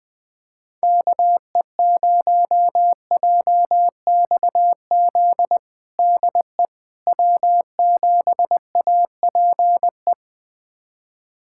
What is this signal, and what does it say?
Morse code, heard as a keyed tone, about 21 wpm, 700 Hz. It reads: KE0JXZ DE W7APE
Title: USA Callsign DE Callsign 20wpm